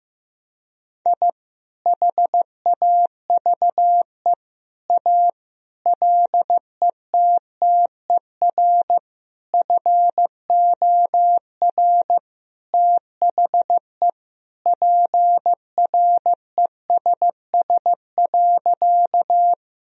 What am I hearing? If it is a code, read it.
Morse code, heard as a keyed tone, about 16 wpm, 700 Hz. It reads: I HAVE A LETTER FOR THE PRESS.
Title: Sentences from Top 1000 Words - Review Part 1 - 15wpm